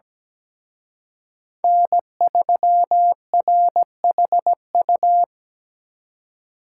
Morse code, heard as a keyed tone, about 20 wpm, 700 Hz. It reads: N3RHU
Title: Speed Racing - USA Call Signs 17wpm